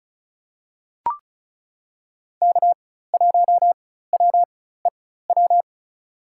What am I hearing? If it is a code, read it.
Morse code, heard as a keyed tone, about 34 wpm, spaced as for 16 wpm, 700 Hz. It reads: K1WEW
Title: USA Call Signs - 4x Character Spacing - 35wpm